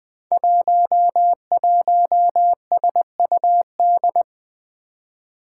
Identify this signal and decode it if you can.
Morse code, heard as a keyed tone, about 20 wpm, 700 Hz. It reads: 11SUD
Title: Five Challenging Letter-Number Combinations 20wpm